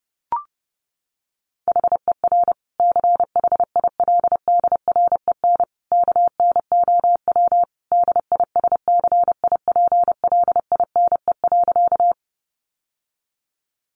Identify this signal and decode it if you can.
Morse code, heard as a keyed tone, about 30 wpm, 700 Hz. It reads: HER CHILDREN KNOW DISCIPLINE.
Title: Sentences from Top 900 Words 30wpm